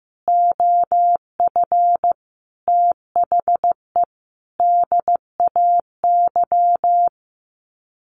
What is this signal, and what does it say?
Morse code, heard as a keyed tone, about 15 wpm, 700 Hz. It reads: OF THE DAY
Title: Sets of 5 Words 15wpm